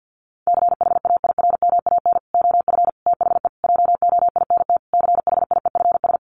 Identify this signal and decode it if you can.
Morse code, heard as a keyed tone, about 50 wpm, 700 Hz. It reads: CHAIRMAN OF THE JOINT CHIEFS
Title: Sets of 5 Words Encore 50wpm